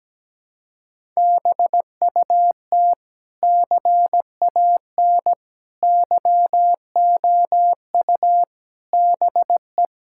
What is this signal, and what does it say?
Morse code, heard as a keyed tone, about 17 wpm, 700 Hz. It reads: BUT CAN YOU BE
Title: Sentences from Top 1000 Words - Review Part 6 - 17wpm